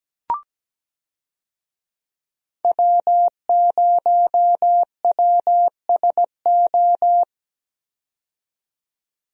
Morse code, heard as a keyed tone, about 17 wpm, 700 Hz. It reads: W0WSO
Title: USA Call Signs - 5 Characters - 17wpm